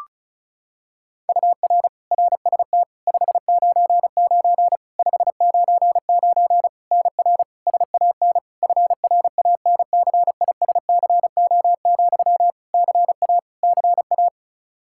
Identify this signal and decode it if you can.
Morse code, heard as a keyed tone, about 36 wpm, 700 Hz. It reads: UR RST 599 599 NR SAN FRANCISCO, CA CA
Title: Practice QSO - Exchange 1 - Beginning - 35wpm